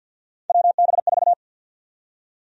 Morse code, heard as a keyed tone, about 50 wpm, 700 Hz. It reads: WB4
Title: USA Call Sign Prefixes - 50wpm